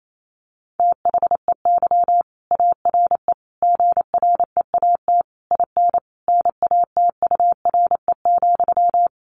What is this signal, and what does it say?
Morse code, heard as a keyed tone, about 28 wpm, 700 Hz. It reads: THEY ARE GREAT IN NATURE,
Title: Sentences from Top 1400 Words 28wpm